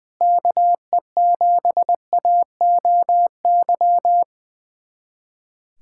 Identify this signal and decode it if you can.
Morse code, heard as a keyed tone, about 20 wpm, 700 Hz. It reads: KE7AOY
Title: USA Callsign DE Callsign 20wpm